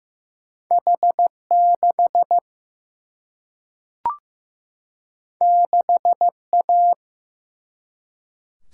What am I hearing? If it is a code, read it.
Morse code, heard as a keyed tone, about 16 wpm, 700 Hz. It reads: H 6 6 A
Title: Single-Letter Single-Number 15wpm